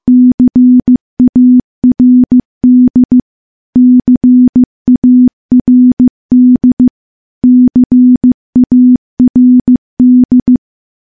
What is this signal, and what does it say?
Morse code, heard as a keyed tone, about 15 wpm, 250 Hz. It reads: CARD CARD CARD
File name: morse / morse-training-15WPM